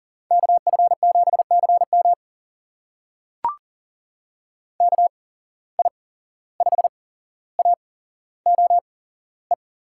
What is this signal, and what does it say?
Morse code, heard as a keyed tone, about 40 wpm, spaced as for 16 wpm, 700 Hz. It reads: KF7CM KI5AOE
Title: USA Call Signs - 8x Character Spacing - 40wpm